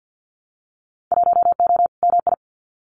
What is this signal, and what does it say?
Morse code, heard as a keyed tone, about 50 wpm, 700 Hz. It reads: 1O MI
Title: Winter Field Day Exchanges 50wpm